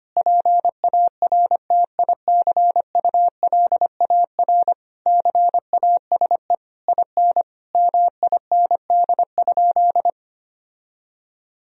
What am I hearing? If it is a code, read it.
Morse code, heard as a keyed tone, about 25 wpm, 700 Hz. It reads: PARTICULAR CASE IN MIND?
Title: Sentences from Top 1000 Words - Review Part 4 - 25wpm